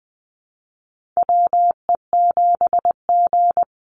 Morse code, heard as a keyed tone, about 20 wpm, 700 Hz. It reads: WE7G
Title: Call Signs Repeated 3x - 20wpm